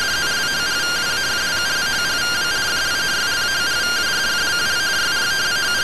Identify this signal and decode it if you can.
Two-tone teleprinter signal, about 170 Hz shift.